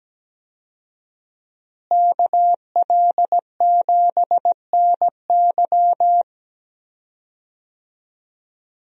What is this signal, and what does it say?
Morse code, heard as a keyed tone, about 17 wpm, 700 Hz. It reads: KL7NY
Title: USA Call Signs - 5 Characters - Repeated 3x - 17wpm